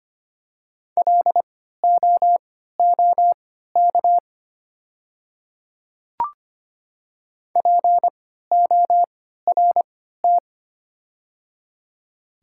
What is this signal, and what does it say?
Morse code, heard as a keyed tone, about 25 wpm, spaced as for 14 wpm, 700 Hz. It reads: LOOK PORT
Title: 3x Character Spacing - Top 200 Words - 25wpm